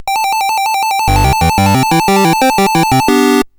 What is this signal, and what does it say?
There is no code.